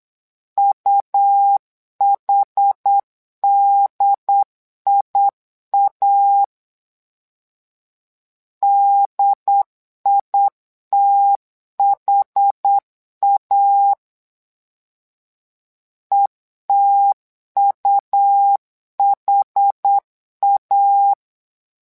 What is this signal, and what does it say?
Morse code, heard as a keyed tone, about 8 wpm, 800 Hz. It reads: UHDIA DITHA ETUHA
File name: LEZIONI / 2 UHDN / 2 UHD 8wpm 2258